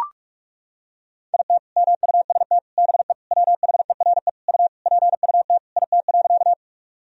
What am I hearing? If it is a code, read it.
Morse code, heard as a keyed tone, about 41 wpm, 700 Hz. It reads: IT MUST BE WHERE U PUT IT.
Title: Sentences from Top 200 Words and Abbreviations 45wpm